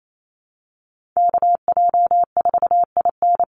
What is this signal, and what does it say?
Morse code, heard as a keyed tone, about 28 wpm, 700 Hz. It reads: KJ4IN
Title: USA Call Signs 28wpm